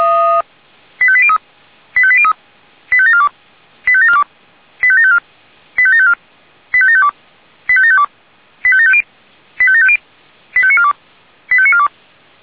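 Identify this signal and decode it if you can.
Five-tone selective call (ZVEI), heard as quick runs of five tones, tones of about 70 ms. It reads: 85803 85803 85742 85742 85754 85754 85752 85752 85750 85750 85942 85942
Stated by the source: Five-tone ZVEI selective call received by SDR.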